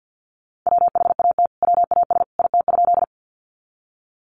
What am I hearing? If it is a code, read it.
Morse code, heard as a keyed tone, about 50 wpm, 700 Hz. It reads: WHAT WAS IT?
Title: Sentences from Top 100 Words 50wpm